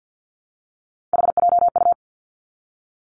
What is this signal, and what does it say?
Morse code, heard as a keyed tone, about 50 wpm, 700 Hz. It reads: HJU
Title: Three Challenging Letter-Number Combinations 50wpm